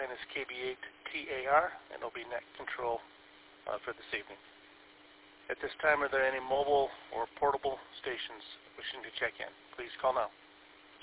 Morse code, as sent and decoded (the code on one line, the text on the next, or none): none